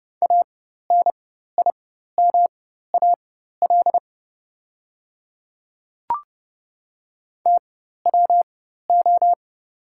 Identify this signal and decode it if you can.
Morse code, heard as a keyed tone, about 30 wpm, spaced as for 14 wpm, 700 Hz. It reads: ANIMAL TWO
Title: 4x Character Spacing - Top 200 Words - 30wpm